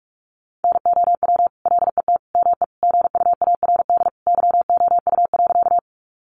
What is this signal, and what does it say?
Morse code, heard as a keyed tone, about 44 wpm, 700 Hz. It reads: NOW LET ME GUARD YOU.
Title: Sentences from Top 700 Words 45wpm